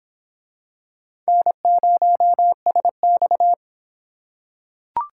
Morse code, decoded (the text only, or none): N0SX